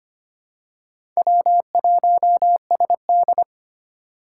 Morse code, heard as a keyed tone, about 25 wpm, 700 Hz. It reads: W1SD
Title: USA Call Signs - 4 Characters - 2x Character Spacing - 25wpm